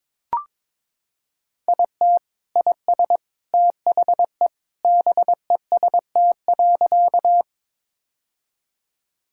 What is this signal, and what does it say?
Morse code, heard as a keyed tone, about 22 wpm, 700 Hz. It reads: IT IS THE BEST.
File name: practice / 22wpm / Sentences with 3-Word Phrases - 22wpm